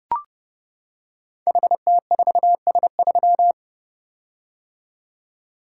Morse code, heard as a keyed tone, about 29 wpm, 700 Hz. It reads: HT4S3
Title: Code Groups - 5 Letter:Numbers - 30wpm